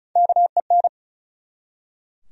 Morse code, heard as a keyed tone, about 36 wpm, 700 Hz. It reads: KEN
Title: USA Common Names - Max 4 Letters - 35wpm